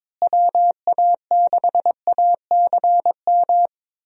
Morse code, heard as a keyed tone, about 22 wpm, 700 Hz. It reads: WA6ACM
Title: USA Call Signs - 6 Characters - Repeated 3x - 22wpm